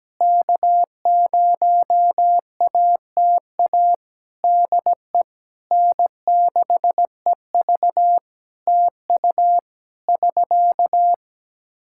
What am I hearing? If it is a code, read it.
Morse code, heard as a keyed tone, about 17 wpm, 700 Hz. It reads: K0ATA DE N6EV TU <SK>
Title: Practice QSO - Exchange 4 - Ending - 17wpm